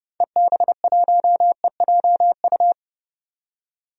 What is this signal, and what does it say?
Morse code, heard as a keyed tone, about 32 wpm, 700 Hz. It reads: EB1EJU
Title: DXCC-100 Call Signs - 6 Characters - 30wpm